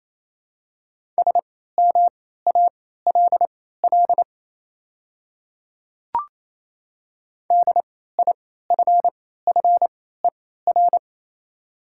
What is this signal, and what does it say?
Morse code, heard as a keyed tone, about 28 wpm, spaced as for 16 wpm, 700 Hz. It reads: SMALL DIFFER
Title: 3x Character Spacing - Top 300 Words - 28wpm